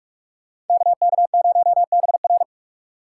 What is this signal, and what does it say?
Morse code, heard as a keyed tone, about 41 wpm, 700 Hz. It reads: KK0BR